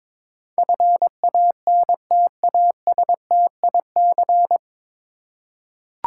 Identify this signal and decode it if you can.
Morse code, heard as a keyed tone, about 22 wpm, 700 Hz. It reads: FANTASTIC
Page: Morse Code Ninja practice 22wpm